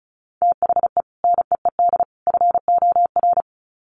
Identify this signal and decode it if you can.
Morse code, heard as a keyed tone, about 34 wpm, 700 Hz. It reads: THE NEED FOR
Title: Sets of 3 Words 35wpm